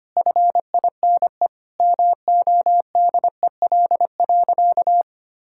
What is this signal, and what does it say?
Morse code, heard as a keyed tone, about 25 wpm, 700 Hz. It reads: FINE MODEL.
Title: Sentences from Top 1000 Words - Review Part 5 - 25wpm